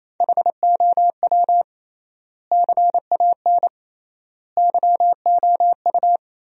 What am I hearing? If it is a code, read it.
Morse code, heard as a keyed tone, about 28 wpm, 700 Hz. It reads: HOW CAN YOU
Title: 3x Word-Spacing - Sentences from Top 200 Words - 28wpm